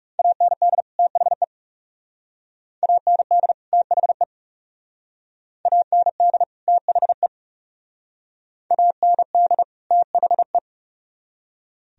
Morse code, heard as a keyed tone, about 38 wpm, 700 Hz. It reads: AND THE AND THE AND THE AND THE